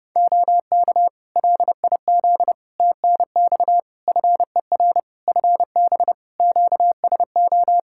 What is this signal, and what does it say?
Morse code, heard as a keyed tone, about 29 wpm, 700 Hz. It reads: OK LIZ TNX FER FB QSO
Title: K1USN SST QSO Practice 30wpm